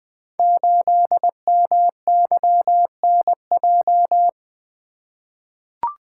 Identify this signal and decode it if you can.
Morse code, heard as a keyed tone, about 20 wpm, 700 Hz. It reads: 8MYNJ
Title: Code Groups - 5 Letter:Numbers - 20wpm